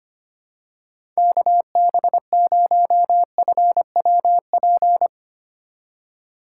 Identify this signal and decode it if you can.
Morse code, heard as a keyed tone, about 25 wpm, 700 Hz. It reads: KB0FWP
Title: USA Call Signs - 6 Characters - 6x Character Spacing - 25wpm